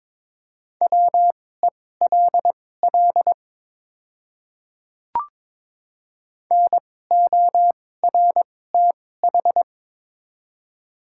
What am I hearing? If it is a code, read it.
Morse code, heard as a keyed tone, about 22 wpm, spaced as for 16 wpm, 700 Hz. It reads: WELL NORTH
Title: 2x Character Spacing - Top 300 Words - 22wpm